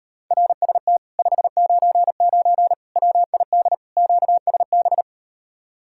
Morse code, heard as a keyed tone, about 38 wpm, 700 Hz. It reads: RST 599 WID QSB